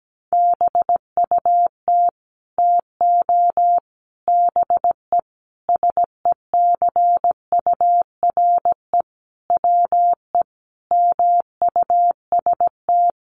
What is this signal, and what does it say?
Morse code, heard as a keyed tone, about 17 wpm, 700 Hz. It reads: BUT TO BE SECURE WE MUST
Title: Sentences from Top 1200 Words 17wpm